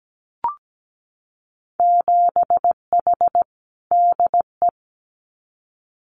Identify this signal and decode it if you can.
Morse code, heard as a keyed tone, about 17 wpm, 700 Hz. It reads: 7H DE
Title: Winter Field Day Exchanges Encore 17wpm